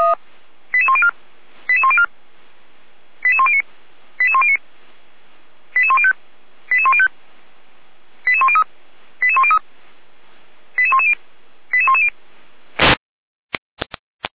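Five-tone selective call (ZVEI), heard as quick runs of five tones, tones of about 70 ms. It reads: 88104 88104 88109 88109 88105 88105 88103 88103 88110 88110
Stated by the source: ZVEI five-tone selective call received on SDR.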